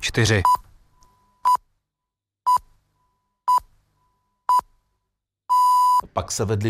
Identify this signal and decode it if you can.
Radio time signal (time pips), with speech. It is heard from 0.4 to 6.0 s.